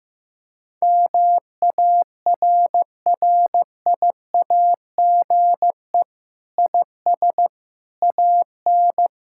Morse code, heard as a keyed tone, about 15 wpm, 700 Hz. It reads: MARRIAGE IS AN
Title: Sentences from Top 1000 Words - Review Part 5 - 15wpm